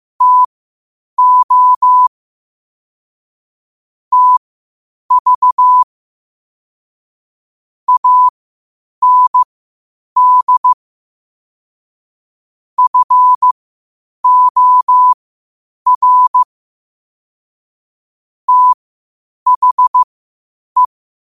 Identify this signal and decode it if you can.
Morse code, heard as a keyed tone, about 15 wpm, spaced as for 8 wpm, 1000 Hz. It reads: TO TV AND FOR THE
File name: qotdfiles / 8wpm